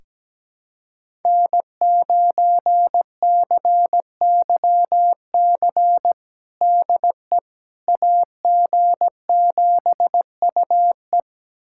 Morse code, heard as a keyed tone, about 17 wpm, 700 Hz. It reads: N9CYC DE AG7UE
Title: USA Callsign DE Callsign - 5 Characters - 17wpm